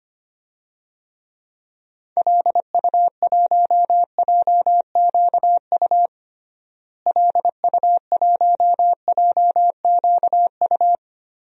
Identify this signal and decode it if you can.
Morse code, heard as a keyed tone, about 25 wpm, 700 Hz. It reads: LU1JQU LU1JQU
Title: DXCC-100 Call Signs - 6 Characters - Repeated 3x - 25wpm